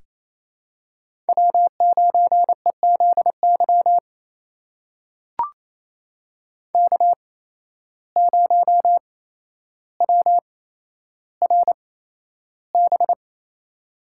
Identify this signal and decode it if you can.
Morse code, heard as a keyed tone, about 28 wpm, spaced as for 12 wpm, 700 Hz. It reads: W9EZY K0WRB
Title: USA Call Signs - 5 Characters - 8x Character Spacing - 28wpm